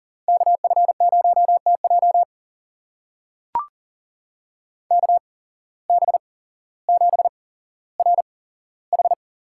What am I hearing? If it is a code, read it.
Morse code, heard as a keyed tone, about 37 wpm, spaced as for 17 wpm, 700 Hz. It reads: KF0TJ KB7RH
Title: USA Call Signs - 5 Characters - 8x Character Spacing - 40wpm